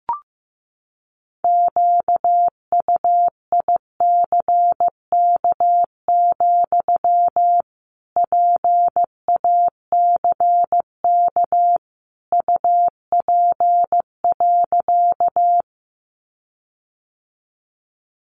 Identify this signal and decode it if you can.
Morse code, heard as a keyed tone, about 15 wpm, 700 Hz. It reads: QUICK, PACK UP.
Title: Sentences from Top 1300 Words 15wpm